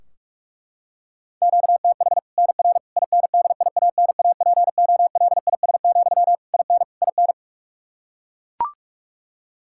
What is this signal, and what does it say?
Morse code, heard as a keyed tone, about 41 wpm, 700 Hz. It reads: QTH NR INDIANAPOLIS, IN IN
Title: Practice QSO - Exchange 1 - Beginning - 45wpm